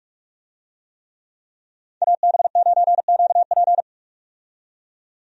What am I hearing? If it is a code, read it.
Morse code, heard as a keyed tone, about 46 wpm, 700 Hz. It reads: AB9QP
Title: USA Call Signs - 5 Characters - 45wpm